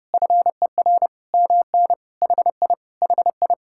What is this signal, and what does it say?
Morse code, heard as a keyed tone, about 30 wpm, 700 Hz. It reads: FER MN HI HI